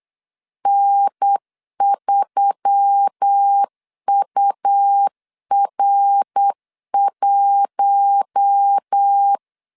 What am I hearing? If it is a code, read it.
Morse code, heard as a keyed tone, about 8 wpm, 800 Hz. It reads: N3UR1
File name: CW / DET / 1 6 1500C S B 8wpm 5142